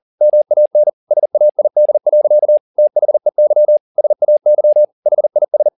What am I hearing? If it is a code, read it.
Morse code, heard as a keyed tone, about 40 wpm, 600 Hz. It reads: MAN SAID. THEY SAY HIS